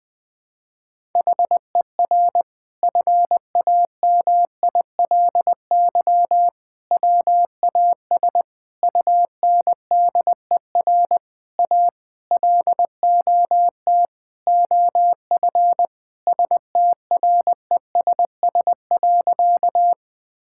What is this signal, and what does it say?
Morse code, heard as a keyed tone, about 20 wpm, 700 Hz. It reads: HER FAMILY WAS UNDER A LOT OF STRESS.
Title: Sentences from Top 1000 Words - Review Part 6 - 20wpm